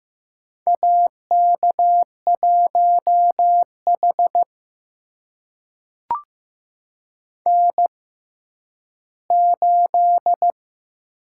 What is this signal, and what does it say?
Morse code, heard as a keyed tone, about 15 wpm, spaced as for 10 wpm, 700 Hz. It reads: AK1H N8
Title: USA Call Signs - 4 Characters - 6x Character Spacing - 15wpm